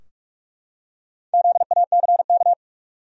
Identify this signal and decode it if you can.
Morse code, heard as a keyed tone, about 45 wpm, 700 Hz. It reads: ZACK
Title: USA Common Names - Max 4 Letters - 45wpm